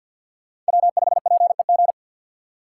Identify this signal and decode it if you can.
Morse code, heard as a keyed tone, about 48 wpm, 700 Hz. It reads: W5PEG
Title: USA Call Signs - 6x Character Spacing - 50wpm